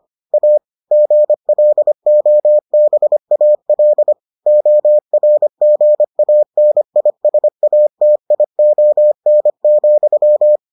Morse code, heard as a keyed tone, about 25 wpm, 600 Hz. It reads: A GLOBAL ORGANISATION,